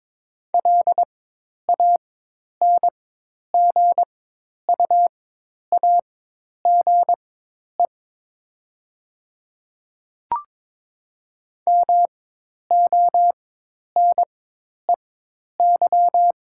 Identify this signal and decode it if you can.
Morse code, heard as a keyed tone, about 22 wpm, spaced as for 10 wpm, 700 Hz. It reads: LANGUAGE MONEY
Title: 4x Character Spacing - Top 500 Words - 22wpm